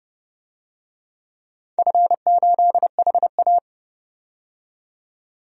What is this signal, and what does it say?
Morse code, heard as a keyed tone, about 29 wpm, 700 Hz. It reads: F8HA
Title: DXCC-100 Call Signs - 4 Characters - 30wpm